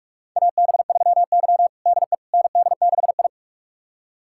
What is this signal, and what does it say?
Morse code, heard as a keyed tone, about 41 wpm, 700 Hz. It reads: AB3Y DE ND6I
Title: USA Callsign DE Callsign - 4 Characters - 45wpm